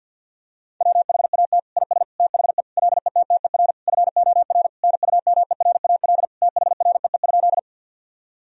Morse code, heard as a keyed tone, about 50 wpm, 700 Hz. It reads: WHAT IS THE LETTER FOR NUMERAL THREE?
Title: Sentences from Top 1000 Words - Review Part 2 - 50wpm